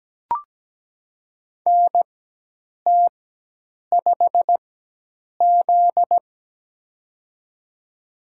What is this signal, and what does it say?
Morse code, heard as a keyed tone, about 17 wpm, spaced as for 8 wpm, 700 Hz. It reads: NT5Z
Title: USA Call Signs - 4x Character Spacing - 17wpm